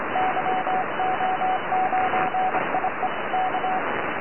Morse code, heard as a keyed tone, about 26 wpm, 750 Hz, with radio noise. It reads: YO9IE K
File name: yo9ie 4m